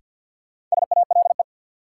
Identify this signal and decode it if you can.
Morse code, heard as a keyed tone, about 50 wpm, 700 Hz. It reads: SALE
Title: Sentences from Top 1400 Words - New Word Drill Alt - 50wpm